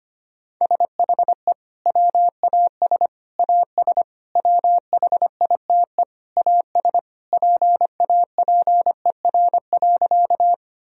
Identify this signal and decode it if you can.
Morse code, heard as a keyed tone, about 25 wpm, 700 Hz. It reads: SHE WAS AS WHITE AS PAPER.
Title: Sentences from Top 1000 Words - Review Part 1 - 25wpm